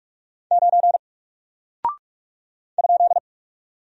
Morse code, heard as a keyed tone, about 44 wpm, 700 Hz. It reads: 9 ?
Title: Single Character (Letter:Number:Punctuation:Prosign) Alt 45wpm